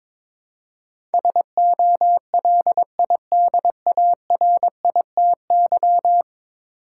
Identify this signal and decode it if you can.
Morse code, heard as a keyed tone, about 22 wpm, 700 Hz. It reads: SOLIDARITY
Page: morse-code-ninja-practice-22wpm